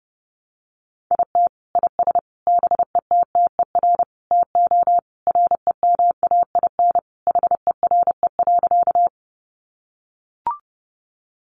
Morse code, heard as a keyed tone, about 30 wpm, 700 Hz. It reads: IT IS BETTER TO REMAIN HERE.
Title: Sentences from Top 1200 Words 30wpm